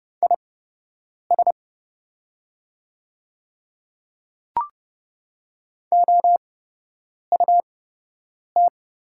Morse code, heard as a keyed tone, about 30 wpm, spaced as for 8 wpm, 700 Hz. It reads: IS OUT
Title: 8x Character Spacing - Top 200 Words - 30wpm